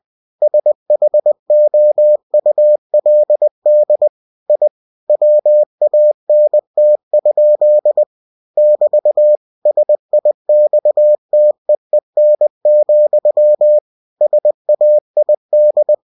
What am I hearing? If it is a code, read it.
Morse code, heard as a keyed tone, about 20 wpm, 600 Hz. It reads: SHOULD I WANT? <BT> SIXTEEN, SAID
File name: cw-books / winnie-the-pooh / winnie-20wpm-page-0060